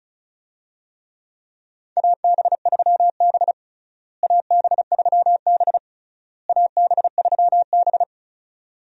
Morse code, heard as a keyed tone, about 34 wpm, 700 Hz. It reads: AB3B AB3B AB3B
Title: USA Call Signs - 4 Characters - Repeated 3x - 35wpm